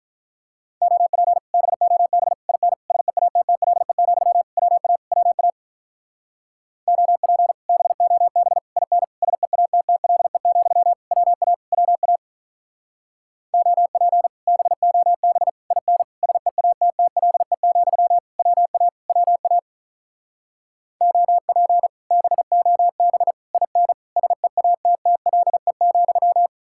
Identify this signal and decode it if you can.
Morse code, heard as a keyed tone, about 48 wpm, 700 Hz. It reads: OP BOB IN SEATTLE, WA WA OP BOB IN SEATTLE, WA WA OP BOB IN SEATTLE, WA WA OP BOB IN SEATTLE,